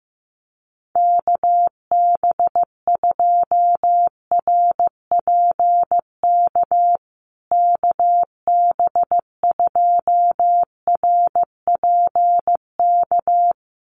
Morse code, heard as a keyed tone, about 14 wpm, 700 Hz. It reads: KB2RPK KB2RPK
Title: USA Call Signs Repeated 2x 15wpm